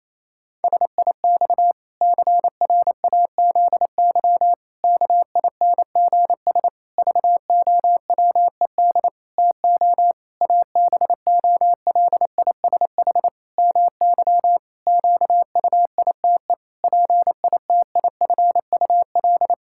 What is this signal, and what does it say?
Morse code, heard as a keyed tone, about 28 wpm, 700 Hz. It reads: SIX CRAZY KINGS VOWED TO ABOLISH MY QUITE PITIFUL
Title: Pangram Challenge v2 - 28wpm